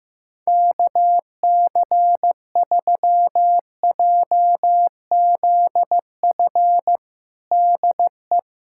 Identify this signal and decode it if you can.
Morse code, heard as a keyed tone, about 15 wpm, 700 Hz. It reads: KC3JZF DE
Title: USA Callsign DE Callsign 15wpm